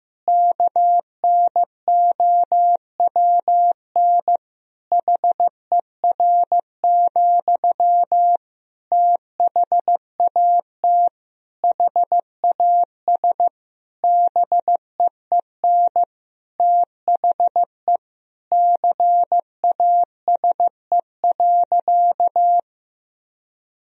Morse code, heard as a keyed tone, about 15 wpm, 700 Hz. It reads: KNOWN HER, THAT HAS BEEN THE CASE.
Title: Sentences from Top 1000 Words - Review Part 3 - 15wpm